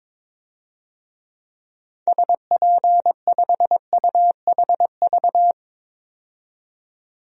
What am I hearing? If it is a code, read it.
Morse code, heard as a keyed tone, about 22 wpm, 700 Hz. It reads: SP5UHV
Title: DXCC-100 Call Signs - 6 Characters - 22wpm